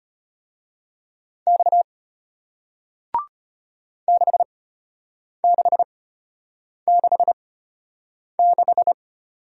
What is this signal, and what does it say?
Morse code, heard as a keyed tone, about 33 wpm, spaced as for 8 wpm, 700 Hz. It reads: X 6666